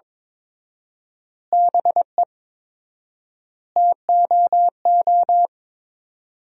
Morse code, heard as a keyed tone, about 23 wpm, 700 Hz. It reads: BE TOO
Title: 4x Word-Spacing - Sentences from Top 300 Words - 22wpm